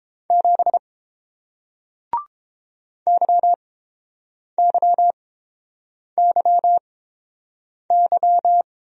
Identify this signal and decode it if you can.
Morse code, heard as a keyed tone, about 29 wpm, spaced as for 7 wpm, 700 Hz. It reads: 7 YYYY